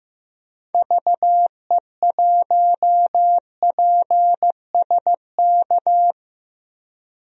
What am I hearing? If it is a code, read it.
Morse code, heard as a keyed tone, about 15 wpm, 700 Hz. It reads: VE1PSK
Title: DXCC-100 Call Signs - 6 Characters - 15wpm